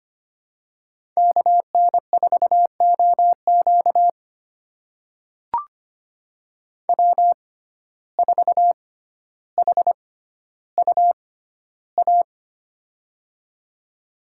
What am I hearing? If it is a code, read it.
Morse code, heard as a keyed tone, about 25 wpm, spaced as for 13 wpm, 700 Hz. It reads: KN4OQ W4HUA
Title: USA Call Signs - 5 Characters - 6x Character Spacing - 25wpm